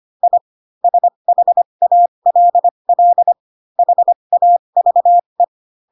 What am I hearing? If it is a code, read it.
Morse code, heard as a keyed tone, about 25 wpm, spaced as for 22 wpm, 700 Hz. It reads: I SHALL HAVE
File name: Pinocchio / Pinocchio ch7 18 25 ik1hge